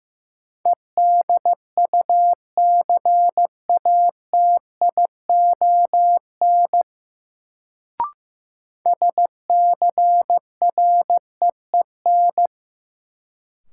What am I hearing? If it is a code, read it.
Morse code, heard as a keyed tone, about 16 wpm, 700 Hz. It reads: EDUCATION SCREEN
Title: Sentences from Top 1000 Words - Review Part 6 - New Word Drill - 15wpm